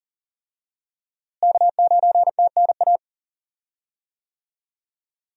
Morse code, heard as a keyed tone, about 37 wpm, 700 Hz. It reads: K9TNA
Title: USA Call Signs - 5 Characters - Repeated 3x - 40wpm